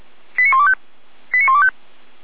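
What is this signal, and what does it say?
Five-tone selective call (ZVEI), heard as quick runs of five tones, tones of about 70 ms. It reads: 89125 89125
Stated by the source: ZVEI five-tone selective call received on an SDR.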